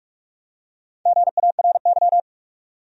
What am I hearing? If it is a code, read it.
Morse code, heard as a keyed tone, about 46 wpm, 700 Hz. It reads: GARY
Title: USA Common Names - Max 5 Letters - 45wpm